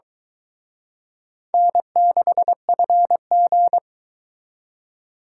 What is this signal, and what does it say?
Morse code, heard as a keyed tone, about 23 wpm, 700 Hz. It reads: N6FG